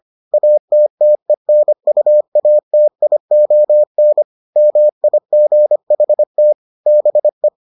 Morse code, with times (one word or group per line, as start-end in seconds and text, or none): 0.3-4.2 ATTENUATION
4.6-6.5 MIGHT
6.9-7.5 BE